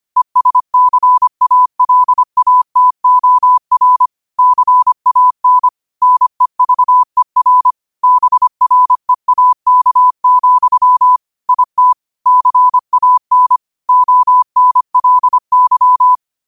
Morse code, heard as a keyed tone, about 25 wpm, 1000 Hz. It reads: ESCALATOR CAN NEVER BREAK, IT CAN ONLY
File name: qotdfiles / 25wpm